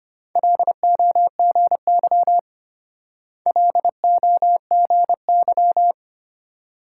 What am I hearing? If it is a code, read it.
Morse code, heard as a keyed tone, about 31 wpm, 700 Hz. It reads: LOGY LOGY
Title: Speed-Racing Multi-letter Phonemes - 25wpm